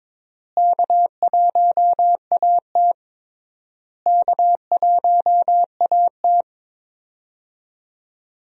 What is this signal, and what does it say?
Morse code, heard as a keyed tone, about 21 wpm, 700 Hz. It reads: K1AT K1AT
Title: USA Call Signs - 4 Characters - Repeated 3x - 22wpm